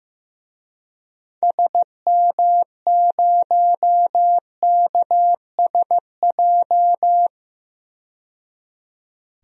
Morse code, heard as a keyed tone, about 15 wpm, 700 Hz. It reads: SM0KSJ
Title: DXCC-100 Call Signs - 6 Characters - 15wpm